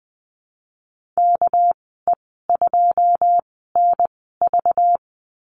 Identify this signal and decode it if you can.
Morse code, heard as a keyed tone, about 20 wpm, spaced as for 14 wpm, 700 Hz. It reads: KE2NV
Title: 231 Lesson 37g - Letter x - Call Signs 20wpm